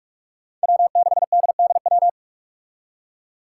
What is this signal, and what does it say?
Morse code, heard as a keyed tone, about 46 wpm, 700 Hz. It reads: W6DDW
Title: Five Challenging Letter-Number Combinations 45wpm